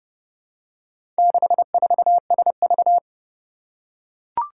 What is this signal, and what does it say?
Morse code, heard as a keyed tone, about 30 wpm, 700 Hz. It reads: 64SV
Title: Four Challenging Letter-Number Combinations 30wpm